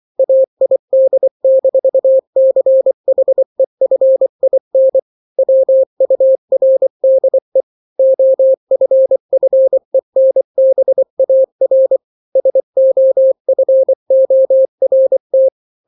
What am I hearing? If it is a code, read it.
Morse code, heard as a keyed tone, about 24 wpm, 530 Hz. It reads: AID-CHEFIN WURDE OFFENBAR SOFORT
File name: daily / 23wpm